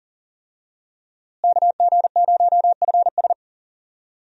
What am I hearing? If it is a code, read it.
Morse code, heard as a keyed tone, about 40 wpm, 700 Hz. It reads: KG0FS
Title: USA Call Signs - 5 Characters - Repeated 2x - 40wpm